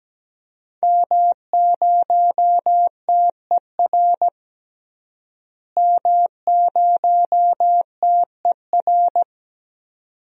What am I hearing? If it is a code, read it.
Morse code, heard as a keyed tone, about 17 wpm, 700 Hz. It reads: M0TER M0TER
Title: DXCC-100 Call Signs - 5 Characters - Repeated 2x - 17wpm